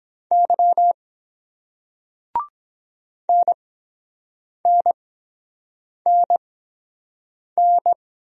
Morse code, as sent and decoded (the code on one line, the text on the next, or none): -.-- / -. -. -. -.
Y NNNN